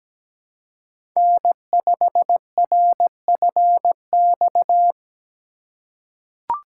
Morse code, heard as a keyed tone, about 17 wpm, 700 Hz. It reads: N5RFX
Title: USA Call Signs - 4x Character Spacing - 17wpm